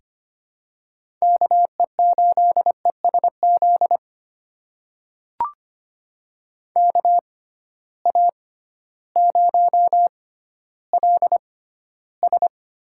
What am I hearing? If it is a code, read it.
Morse code, heard as a keyed tone, about 25 wpm, spaced as for 13 wpm, 700 Hz. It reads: KE8ESZ KA0LS
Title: USA Call Signs - 6x Character Spacing - 25wpm